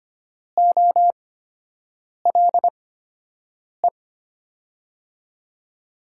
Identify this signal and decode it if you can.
Morse code, heard as a keyed tone, about 25 wpm, spaced as for 7 wpm, 700 Hz. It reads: OLE
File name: Top 90 QSO Elements - 8x Character Spacing - 25wpm